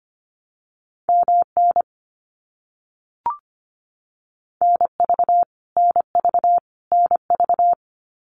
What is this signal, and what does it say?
Morse code, heard as a keyed tone, about 25 wpm, 700 Hz. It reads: MN NV NV NV
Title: US States Repeated 3x 25wpm